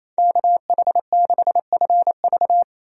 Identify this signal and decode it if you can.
Morse code, heard as a keyed tone, about 28 wpm, 700 Hz. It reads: KH6FV
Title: USA Call Signs - 5 Characters - Repeated 2x - 28wpm